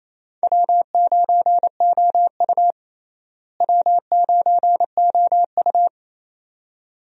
Morse code, heard as a keyed tone, about 28 wpm, 700 Hz. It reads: W9OU W9OU
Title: USA Call Signs - 4 Characters - Repeated 2x - 28wpm